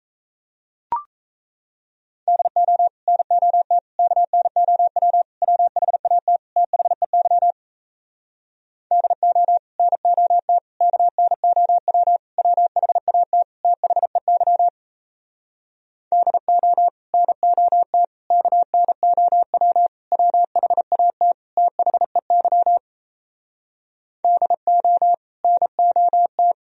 Morse code, heard as a keyed tone, about 37 wpm, 700 Hz. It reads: DO NOT KNOW WHAT THEY DO NOT KNOW WHAT THEY DO NOT KNOW WHAT THEY DO NOT